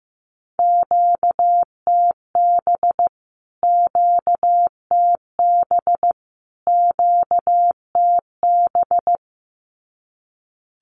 Morse code, heard as a keyed tone, about 15 wpm, 700 Hz. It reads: QTB QTB QTB
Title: Q-Codes Repeated 3x 15wpm